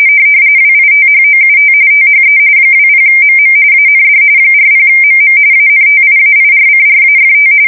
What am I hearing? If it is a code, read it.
RTTY, about 45.45 baud. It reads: WITHOUT THE 20MM AND 40MM ANTIAIRCRAFT
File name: fsk170 / battleshipnj